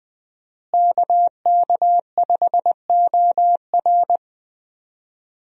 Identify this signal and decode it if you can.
Morse code, heard as a keyed tone, about 21 wpm, 700 Hz. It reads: KK5OR
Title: USA Call Signs - 5 Characters - Repeated 3x - 20wpm